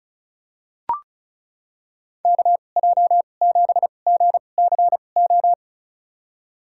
Morse code, heard as a keyed tone, about 34 wpm, spaced as for 25 wpm, 700 Hz. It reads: KJ7GCO
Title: USA Call Signs - 6 Characters - 2x Character Spacing - 35wpm